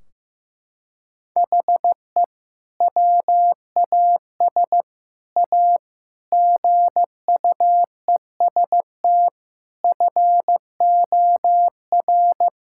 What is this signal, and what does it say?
Morse code, heard as a keyed tone, about 15 wpm, 700 Hz. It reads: HE WAS A GUEST FOR
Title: Sentences from Top 1600 Words 15wpm